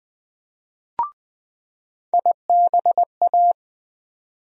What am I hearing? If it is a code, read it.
Morse code, heard as a keyed tone, about 20 wpm, 700 Hz. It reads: IBA
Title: USA Call Sign Postfixes - 3 Letters 20wpm